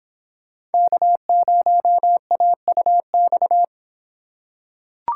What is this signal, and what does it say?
Morse code, heard as a keyed tone, about 23 wpm, 700 Hz. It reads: K0AUX